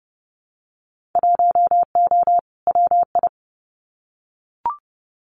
Morse code, heard as a keyed tone, about 30 wpm, 700 Hz. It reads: 1O WI
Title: Winter Field Day Exchanges 30wpm